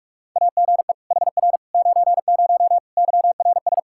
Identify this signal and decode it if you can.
Morse code, heard as a keyed tone, about 46 wpm, 700 Hz. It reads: AGE HR 90 YRS
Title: Practice QSO - Exchange 3 - More Info - 45wpm